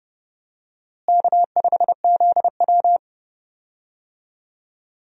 Morse code, heard as a keyed tone, about 32 wpm, 700 Hz. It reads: K5ZW
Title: USA Call Signs - 4 Characters - Repeated 3x - 30wpm